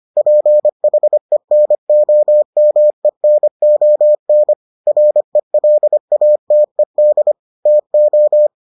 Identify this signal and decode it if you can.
Morse code, heard as a keyed tone, about 25 wpm, 600 Hz. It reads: PHENOMENON RELATED TO